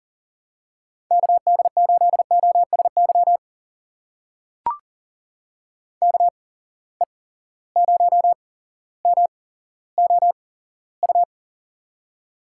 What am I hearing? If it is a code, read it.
Morse code, heard as a keyed tone, about 39 wpm, spaced as for 17 wpm, 700 Hz. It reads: KD8OSY KE0MOU
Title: USA Call Signs - 6 Characters - 8x Character Spacing - 40wpm